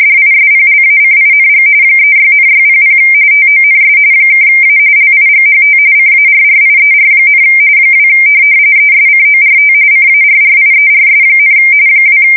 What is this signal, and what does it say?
RTTY, about 45.45 baud. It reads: DISPLACING 36,600 TONS (A LIGHT CARRIER, 2 ESCORT CARRIERS, 2
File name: fsk170 / USS-New-Jersey